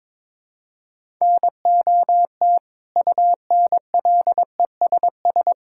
Morse code, heard as a keyed tone, about 22 wpm, 700 Hz. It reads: NOT UNLESS
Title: Sentences from Top 1000 Words - Review Part 6 - 22wpm